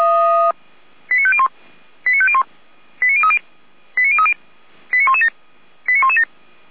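Five-tone selective call (ZVEI), heard as quick runs of five tones, tones of about 70 ms. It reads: 89501 89501 89030 89030 89117 89117
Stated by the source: ZVEI five-tone call 89092 received by SDR.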